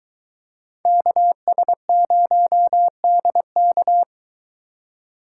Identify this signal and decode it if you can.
Morse code, heard as a keyed tone, about 23 wpm, 700 Hz. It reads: KS0DK